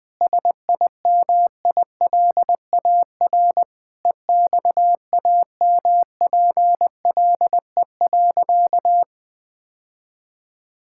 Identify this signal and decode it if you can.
Morse code, heard as a keyed tone, about 20 wpm, 700 Hz. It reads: SIMILAR EXAMPLE.
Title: Sentences from Top 1000 Words - Review Part 5 - 20wpm